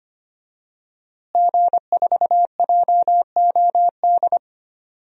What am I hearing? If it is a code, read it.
Morse code, heard as a keyed tone, about 25 wpm, 700 Hz. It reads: G4JOD
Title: DXCC-100 Call Signs - 5 Characters - Repeated 3x - 25wpm